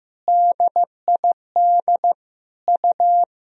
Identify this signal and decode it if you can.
Morse code, heard as a keyed tone, about 15 wpm, 700 Hz. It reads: DID U
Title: Sentences from Top 200 Words and Abbreviations 15wpm